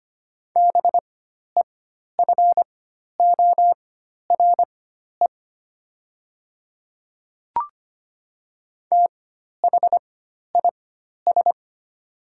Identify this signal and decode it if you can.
Morse code, heard as a keyed tone, about 25 wpm, spaced as for 12 wpm, 700 Hz. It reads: BEFORE THIS
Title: 4x Character Spacing - Top 500 Words - 25wpm